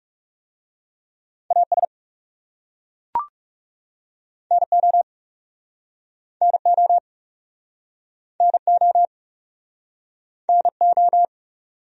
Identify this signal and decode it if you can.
Morse code, heard as a keyed tone, about 40 wpm, 700 Hz. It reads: AS NO NO NO NO